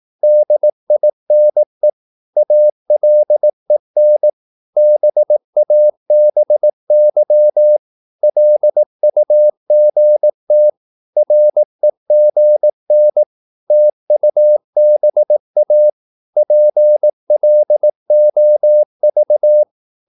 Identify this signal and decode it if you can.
Morse code, heard as a keyed tone, about 18 wpm, 600 Hz. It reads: DINE ALEN BABY LUGT REGN TUBA PLOV
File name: Korte ord DK 18wpm